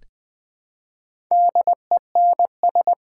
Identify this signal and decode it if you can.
Morse code, heard as a keyed tone, about 20 wpm, 700 Hz. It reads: DENS
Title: Build A Word - 1 Letter At A Time - 3 to 5-Letter Words - 20wpm